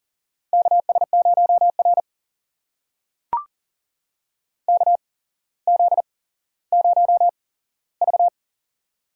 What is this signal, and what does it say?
Morse code, heard as a keyed tone, about 37 wpm, spaced as for 17 wpm, 700 Hz. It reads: KS0R KZ0V
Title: USA Call Signs - 4 Characters - 8x Character Spacing - 40wpm